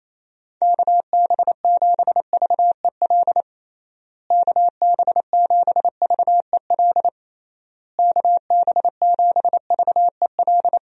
Morse code, heard as a keyed tone, about 28 wpm, 700 Hz. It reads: KB7VEL KB7VEL KB7VEL
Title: USA Call Signs - 6 Characters - Repeated 3x - 28wpm